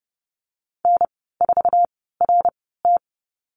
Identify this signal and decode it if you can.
Morse code, heard as a keyed tone, about 30 wpm, spaced as for 17 wpm, 700 Hz. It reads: N4RT
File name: Lesson 14g - Number 2 - Call Signs 30wpm